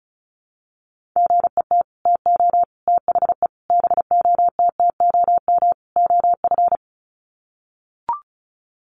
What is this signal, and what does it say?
Morse code, heard as a keyed tone, about 34 wpm, 700 Hz. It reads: GET TO THE BOTTOM OF
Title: Sets of 5 Words Encore 35wpm